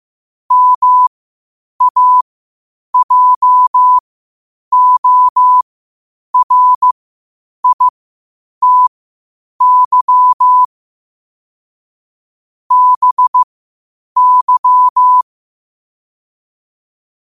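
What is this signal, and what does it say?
Morse code, heard as a keyed tone, about 15 wpm, spaced as for 8 wpm, 1000 Hz. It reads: MAJORITY BY
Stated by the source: Morse practice at 8 WPM from Morse Resource.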